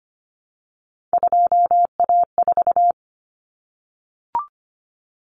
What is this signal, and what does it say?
Morse code, heard as a keyed tone, about 25 wpm, 700 Hz. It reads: 2A4
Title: Common Cut Numbers - 3-Digit 25wpm